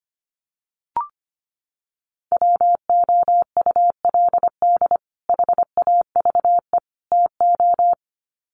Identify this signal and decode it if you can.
Morse code, heard as a keyed tone, about 25 wpm, 700 Hz. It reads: WOULD HAVE TO
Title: Sets of 3 Words 25wpm